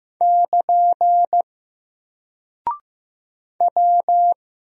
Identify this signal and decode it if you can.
Morse code, heard as a keyed tone, about 15 wpm, 700 Hz. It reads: <KN> W
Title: Single Character (Letter:Number:Punctuation:Prosign) Alt 15wpm